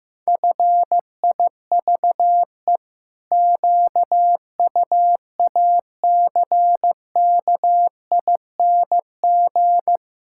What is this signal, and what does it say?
Morse code, heard as a keyed tone, about 15 wpm, 700 Hz. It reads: FIVE QUACKING
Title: Pangram Challenge v2 - 15wpm